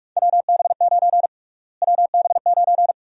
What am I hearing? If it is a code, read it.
Morse code, heard as a keyed tone, about 46 wpm, 700 Hz. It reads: WB9 WB9
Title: USA Call Sign Prefixes - Repeated 2x - 45wpm